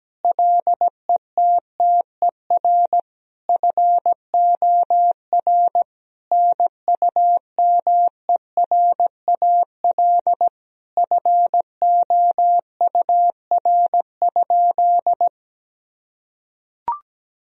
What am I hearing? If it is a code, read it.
Morse code, heard as a keyed tone, about 17 wpm, 700 Hz. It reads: LETTER FOR NUMERAL FOUR?
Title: Sentences from Top 1000 Words - Review Part 2 - 17wpm